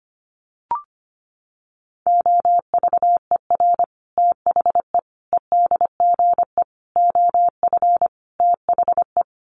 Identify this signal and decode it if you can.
Morse code, heard as a keyed tone, about 25 wpm, 700 Hz. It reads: OVER THE EDGE OF THE
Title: Sets of 5 Words Encore2 25wpm